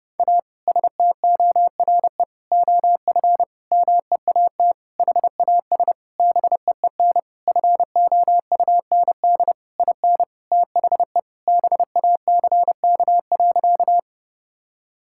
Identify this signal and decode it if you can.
Morse code, heard as a keyed tone, about 31 wpm, 700 Hz. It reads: A STORE OF MEAT HAS BEEN FOUND IN THE BACK.
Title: Sentences from Top 1000 Words - Review Part 5 - 30wpm